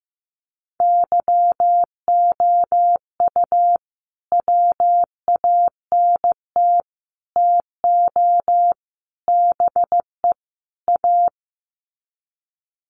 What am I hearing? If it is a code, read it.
Morse code, heard as a keyed tone, about 15 wpm, 700 Hz. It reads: YOU WANT TO BE A
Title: Sets of 5 Words Encore2 15wpm